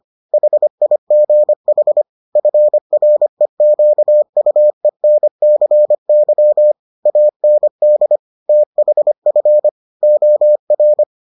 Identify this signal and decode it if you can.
Morse code, heard as a keyed tone, about 25 wpm, 600 Hz. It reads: HIGH FREQUENCY AND THF OR